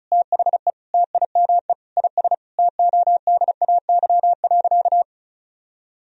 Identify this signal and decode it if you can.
Morse code, heard as a keyed tone, about 36 wpm, 700 Hz. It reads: THE TIME IS TODAY.
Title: Sentences from Top 1000 Words - Review Part 2 - 35wpm